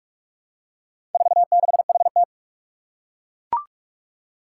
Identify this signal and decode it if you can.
Morse code, heard as a keyed tone, about 41 wpm, 700 Hz. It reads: 46HT
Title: Four Challenging Letter-Number Combinations 45wpm